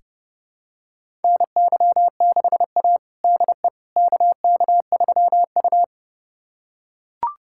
Morse code, heard as a keyed tone, about 30 wpm, 700 Hz. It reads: NY6A DE KK3U
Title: USA Callsign DE Callsign - 4 Characters - 30wpm